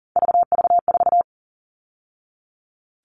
Morse code, heard as a keyed tone, about 40 wpm, 700 Hz. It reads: VV4
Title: All Cut Numbers - 3-Digit 40wpm